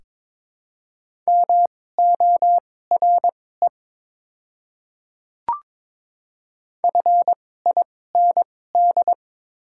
Morse code, heard as a keyed tone, about 22 wpm, spaced as for 16 wpm, 700 Hz. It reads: MORE FIND
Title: 2x Character Spacing - Top 300 Words - 22wpm